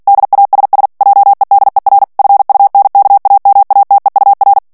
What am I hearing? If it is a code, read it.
Morse code, heard as a keyed tone, about 48 wpm, 800 Hz. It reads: DASS JEDER FUNKAMATEUR
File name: morse mp3 / ALLEMAND texte clair 2 vitesse 45 mots minute